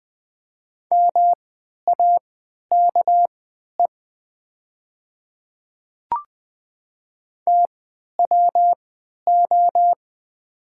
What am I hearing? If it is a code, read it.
Morse code, heard as a keyed tone, about 20 wpm, spaced as for 11 wpm, 700 Hz. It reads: MAKE TWO
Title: 3x Character Spacing - Top 100 Words - 20wpm